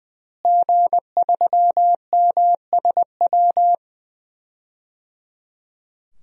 Morse code, heard as a keyed tone, about 20 wpm, 700 Hz. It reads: G3MSW
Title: DXCC-100 Call Signs - 5 Characters - Repeated 3x - 20wpm